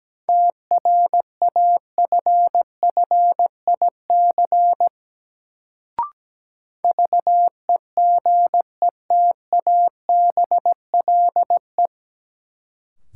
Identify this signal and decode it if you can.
Morse code, heard as a keyed tone, about 17 wpm, 700 Hz. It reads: TRAFFIC VEGETABLE
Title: Sentences from Top 1300 Words - New Word Drill - 17wpm